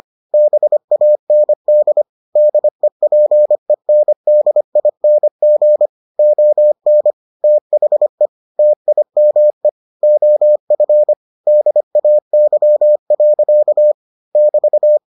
Morse code, heard as a keyed tone, about 25 wpm, 600 Hz. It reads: BAND DEPENDING ON THE TIME OF DAY. <BT>